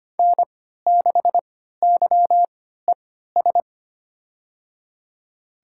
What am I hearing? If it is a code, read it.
Morse code, heard as a keyed tone, about 25 wpm, spaced as for 14 wpm, 700 Hz. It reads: N6YES
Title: USA Call Signs - 3x Character Spacing - 25wpm